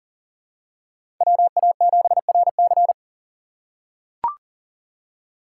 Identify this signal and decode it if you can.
Morse code, heard as a keyed tone, about 40 wpm, 700 Hz. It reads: WA7RC
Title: USA Call Signs - 5 Characters - 6x Character Spacing - 40wpm